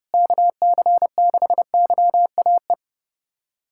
Morse code, heard as a keyed tone, about 30 wpm, 700 Hz. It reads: KC6YAE
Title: USA Callsign DE Callsign 30wpm